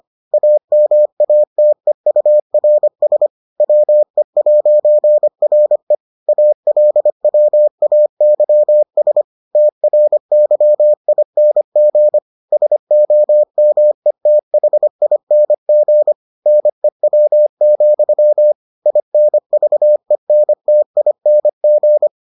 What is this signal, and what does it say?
Morse code, heard as a keyed tone, about 25 wpm, 600 Hz. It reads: AMATEURS WE'RE ALWAYS TRYING SOMETHING NEW, INVENTING